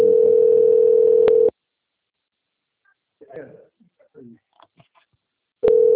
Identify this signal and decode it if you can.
Ringback tone (North America), 440 + 480 Hz, with speech.
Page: Toledo - PR